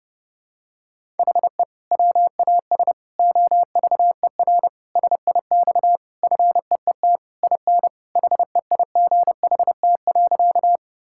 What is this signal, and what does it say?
Morse code, heard as a keyed tone, about 30 wpm, 700 Hz. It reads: HE WAS OVER SIX FEET IN HEIGHT.
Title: Sentences from Top 1600 Words 30wpm